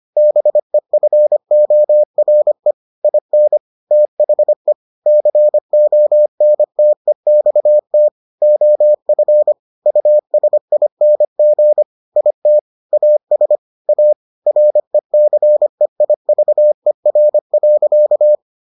Morse code, heard as a keyed tone, about 25 wpm, 600 Hz. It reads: BEFORE IN THE CONTEXT OF USING IT AS A RECEIVER.